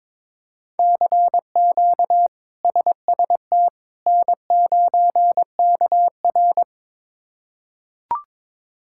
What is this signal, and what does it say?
Morse code, heard as a keyed tone, about 22 wpm, 700 Hz. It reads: CQ SST N9KR
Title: K1USN SST QSO Practice - 22wpm